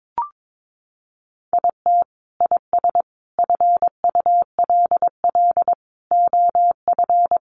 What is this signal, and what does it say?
Morse code, heard as a keyed tone, about 22 wpm, 700 Hz. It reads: IT IS FULL OF
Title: Sentences from Top 1500 Words 22wpm